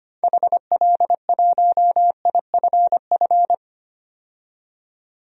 Morse code, heard as a keyed tone, about 25 wpm, 700 Hz. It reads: HL1IFF
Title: DXCC-100 Call Signs - 6 Characters - 25wpm